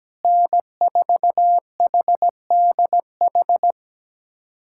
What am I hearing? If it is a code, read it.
Morse code, heard as a keyed tone, about 17 wpm, 700 Hz. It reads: N4HDH
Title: USA Callsign DE Callsign - 5 Characters - 17wpm